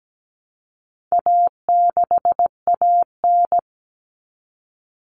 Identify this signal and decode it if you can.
Morse code, heard as a keyed tone, about 17 wpm, 700 Hz. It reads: A6AN
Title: Common Cut Numbers - 4-Digit 17wpm